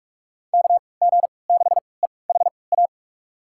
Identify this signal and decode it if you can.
Morse code, heard as a keyed tone, about 42 wpm, spaced as for 25 wpm, 700 Hz. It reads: KG6EHA
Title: USA Call Signs - 6 Characters - 3x Character Spacing - 45wpm